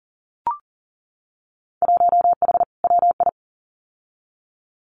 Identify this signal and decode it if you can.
Morse code, heard as a keyed tone, about 41 wpm, 700 Hz. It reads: 1H WI
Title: Winter Field Day Exchanges 40wpm